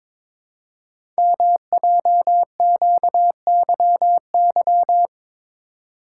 Morse code, heard as a keyed tone, about 23 wpm, 700 Hz. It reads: MJQYY
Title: Code Groups - 5 Letters - 22wpm